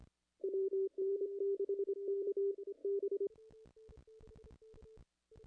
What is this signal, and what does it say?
Morse code, heard as a keyed tone, about 25 wpm, 410 Hz. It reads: WN6K B
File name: contests / ss-cw-k5tr-op-n5rz -0100